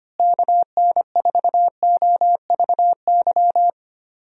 Morse code, heard as a keyed tone, about 25 wpm, 700 Hz. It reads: KN4OVY
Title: USA Call Signs - 6 Characters - Repeated 3x - 25wpm